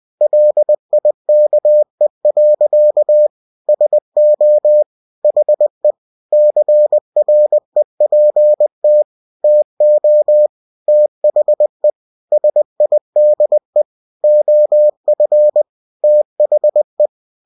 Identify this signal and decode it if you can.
Morse code, heard as a keyed tone, about 20 wpm, 600 Hz. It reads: LIKE. SO HE CREPT TO THE SIDE OF THE
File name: cw-books / winnie-the-pooh / winnie-20wpm-page-0042